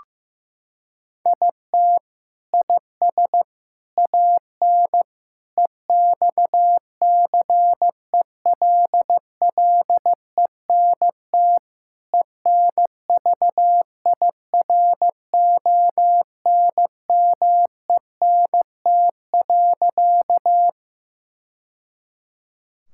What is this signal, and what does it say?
Morse code, heard as a keyed tone, about 15 wpm, 700 Hz. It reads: IT IS AN EXCELLENT ENVIRONMENT.
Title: Sentences from Top 1000 Words - Review Part 5 - 15wpm